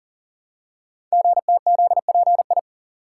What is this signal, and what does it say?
Morse code, heard as a keyed tone, about 42 wpm, 700 Hz. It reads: GTZPI
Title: Code Groups - 5 Letters - 40wpm